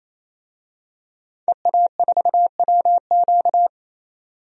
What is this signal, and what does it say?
Morse code, heard as a keyed tone, about 28 wpm, 700 Hz. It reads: EA4WQ
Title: DXCC-100 Call Signs - 5 Characters - 28wpm